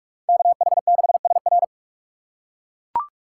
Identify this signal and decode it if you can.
Morse code, heard as a keyed tone, about 41 wpm, 700 Hz. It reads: KH6SR